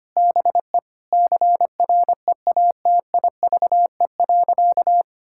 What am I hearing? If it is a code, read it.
Morse code, heard as a keyed tone, about 25 wpm, 700 Hz. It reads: BE CREATIVE.
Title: Sentences from Top 1000 Words - Review Part 5 - 25wpm